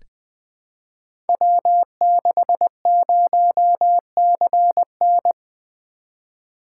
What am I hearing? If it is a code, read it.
Morse code, heard as a keyed tone, about 20 wpm, 700 Hz. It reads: W60CN
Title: Code Groups - 5 Letter:Numbers - 20wpm